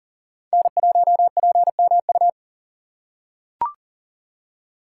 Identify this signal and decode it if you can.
Morse code, heard as a keyed tone, about 40 wpm, 700 Hz. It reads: N1PMU
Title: USA Call Signs - 5 Characters - 3x Character Spacing - 40wpm